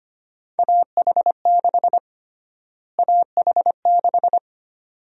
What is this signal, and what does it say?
Morse code, heard as a keyed tone, about 25 wpm, 700 Hz. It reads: AH6 AH6
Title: USA Call Sign Prefixes - Repeated 2x - 25wpm